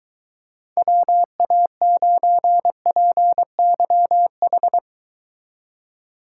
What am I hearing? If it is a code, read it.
Morse code, heard as a keyed tone, about 23 wpm, 700 Hz. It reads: WA9PYH